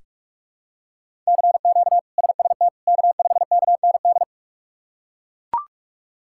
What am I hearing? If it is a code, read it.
Morse code, heard as a keyed tone, about 42 wpm, 700 Hz. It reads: CQ SST K5KND
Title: K1USN SST QSO Practice - 45wpm